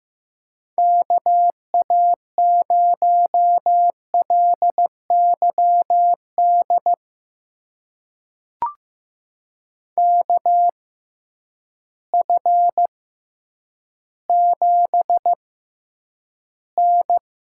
Morse code, heard as a keyed tone, about 15 wpm, spaced as for 9 wpm, 700 Hz. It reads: KA0LYD KF7N
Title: USA Call Signs - 6 Characters - 6x Character Spacing - 15wpm